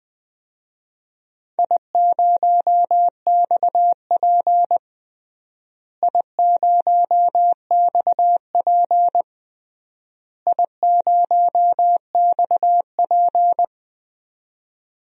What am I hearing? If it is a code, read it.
Morse code, heard as a keyed tone, about 20 wpm, 700 Hz. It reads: I0XP I0XP I0XP
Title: DXCC-100 Call Signs - 4 Characters - Repeated 3x - 20wpm